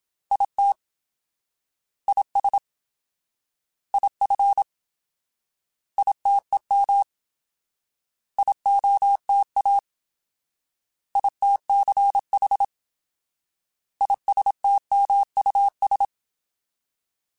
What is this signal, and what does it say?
Morse code, heard as a keyed tone, about 27 wpm, 800 Hz. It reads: IT IS IF ITEM IOTA ITCH ISTMUS
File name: Alphabetical / Alphabetical mp3s / Is @25